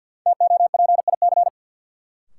Morse code, heard as a keyed tone, about 51 wpm, 700 Hz. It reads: TOPIC
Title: Sentences from Top 1200 Words - New Word Drill - 50wpm